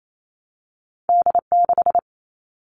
Morse code, heard as a keyed tone, about 28 wpm, 700 Hz. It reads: D6
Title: All Cut Numbers - 2-Digit 28wpm